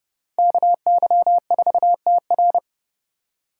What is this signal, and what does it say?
Morse code, heard as a keyed tone, about 28 wpm, 700 Hz. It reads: KY4TR